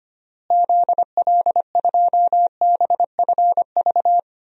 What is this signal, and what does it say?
Morse code, heard as a keyed tone, about 25 wpm, 700 Hz. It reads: ZL2BFV
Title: DXCC-100 Call Signs - 6 Characters - Repeated 3x - 25wpm